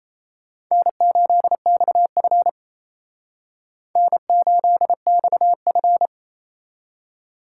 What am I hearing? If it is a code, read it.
Morse code, heard as a keyed tone, about 35 wpm, 700 Hz. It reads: N8XF N8XF
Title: Speed Racing - USA Call Signs 28wpm